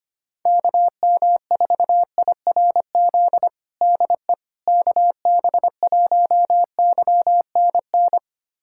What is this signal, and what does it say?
Morse code, heard as a keyed tone, about 25 wpm, 700 Hz. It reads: KM4IRZ DE KB1YNN
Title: USA Callsign DE Callsign - 6 Characters - 25wpm